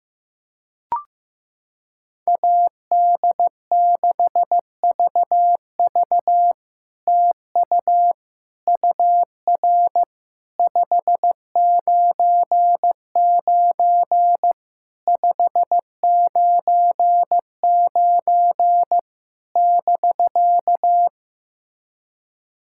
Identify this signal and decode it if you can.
Morse code, heard as a keyed tone, about 15 wpm, 700 Hz. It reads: AD6VV TU UR 599 599 <BK>
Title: SOTA QSO Practice 15wpm